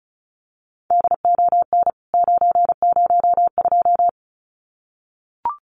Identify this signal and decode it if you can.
Morse code, heard as a keyed tone, about 34 wpm, 700 Hz. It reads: DON 902
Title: CWT Exchange 35wpm